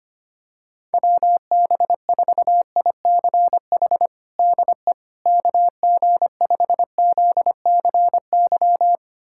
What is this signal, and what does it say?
Morse code, heard as a keyed tone, about 25 wpm, 700 Hz. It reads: WB4ICH DE KG5ZCY
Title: USA Callsign DE Callsign - 6 Characters - 25wpm